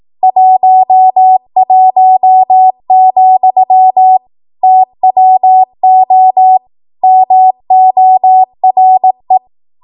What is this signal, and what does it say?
Morse code, heard as a keyed tone, about 18 wpm, 750 Hz. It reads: 11, TWO MORE
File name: Archive / 18 WPM / 18WPM